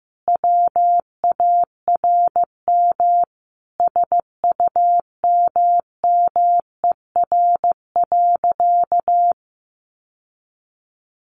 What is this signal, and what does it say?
Morse code, heard as a keyed tone, about 15 wpm, 700 Hz. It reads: WARM SUMMER.
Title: Sentences from Top 1100 Words 15wpm